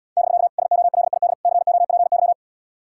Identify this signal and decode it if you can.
Morse code, heard as a keyed tone, about 77 wpm, 700 Hz. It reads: <BK> SOLID COPY